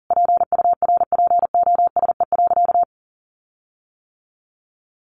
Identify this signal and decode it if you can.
Morse code, heard as a keyed tone, about 40 wpm, 700 Hz. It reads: PURPOSE.
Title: Sentences from Top 1400 Words 40wpm